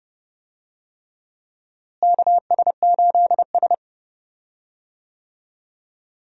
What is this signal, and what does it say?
Morse code, heard as a keyed tone, about 32 wpm, 700 Hz. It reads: KS8S
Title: USA Call Signs - 4 Characters - Repeated 3x - 30wpm